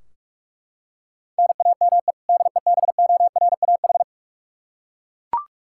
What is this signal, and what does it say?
Morse code, heard as a keyed tone, about 41 wpm, 700 Hz. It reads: NAME DEBORAH
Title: Practice QSO - Exchange 1 - Beginning - 45wpm